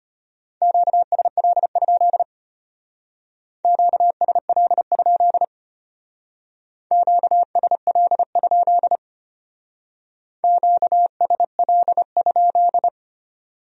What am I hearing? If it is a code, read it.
Morse code, heard as a keyed tone, about 30 wpm, 700 Hz. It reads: QSL? QSL? QSL? QSL?
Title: Speed-Racing Q-Codes - 25wpm